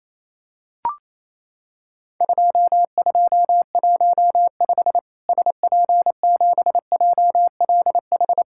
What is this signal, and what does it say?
Morse code, heard as a keyed tone, about 28 wpm, 700 Hz. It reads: 2215 SP7JLH
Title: Worked All Europe DX Contest - QTC Exchange - 28wpm